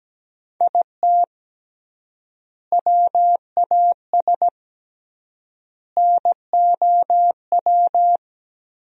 Morse code, heard as a keyed tone, about 17 wpm, 700 Hz. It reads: IT WAS NOW
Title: 3x Word-Spacing - Sentences from Top 200 Words - 17wpm